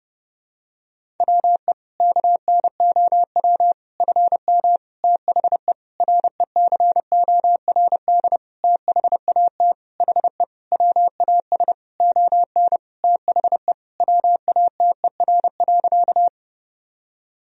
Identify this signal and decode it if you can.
Morse code, heard as a keyed tone, about 31 wpm, 700 Hz. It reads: WE KNOW FM THE RECORD THAT HE WAS ON THE WATER.
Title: Sentences from Top 500 Words and Abbreviations 30wpm